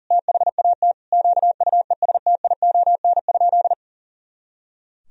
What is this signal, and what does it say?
Morse code, heard as a keyed tone, about 37 wpm, 700 Hz. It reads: THAT QUESTION?
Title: Sentences from Top 400 Words and Abbreviations 40wpm